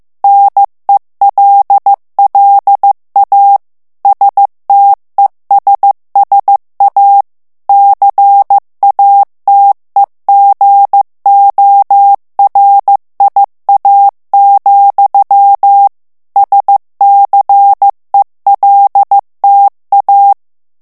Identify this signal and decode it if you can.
Morse code, heard as a keyed tone, about 15 wpm, 800 Hz. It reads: NELLA STESSA CATEGORIA, SCELTA
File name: morse mp3 / ITALIEN texte clair 2 vitesse 14 mots minute